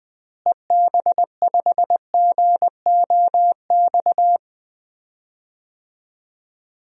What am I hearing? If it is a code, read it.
Morse code, heard as a keyed tone, about 20 wpm, 700 Hz. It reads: EB5GOX
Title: DXCC-100 Call Signs - 6 Characters - Repeated 3x - 20wpm